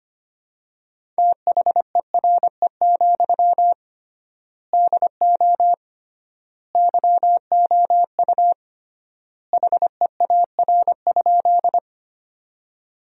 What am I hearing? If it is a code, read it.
Morse code, heard as a keyed tone, about 25 wpm, 700 Hz. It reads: THERE, DO YOU HEAR?
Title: 3x Word-Spacing - Sentences from Top 300 Words - 25wpm